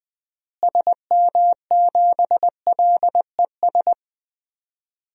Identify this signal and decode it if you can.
Morse code, heard as a keyed tone, about 21 wpm, 700 Hz. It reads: SM7LES
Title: DXCC-100 Call Signs - 6 Characters - Repeated 2x - 20wpm